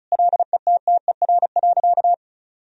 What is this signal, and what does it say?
Morse code, heard as a keyed tone, about 36 wpm, 700 Hz. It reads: LETTER.
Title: Sentences from Top 1000 Words - Review Part 4 - 35wpm